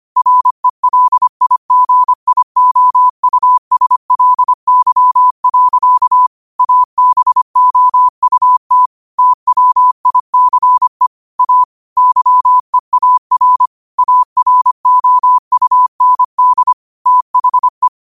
Morse code, heard as a keyed tone, about 25 wpm, 1000 Hz. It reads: RELIGIOUSLY. ABOUT TWICE A YEAR AROUND THE
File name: qotdfiles / 25wpm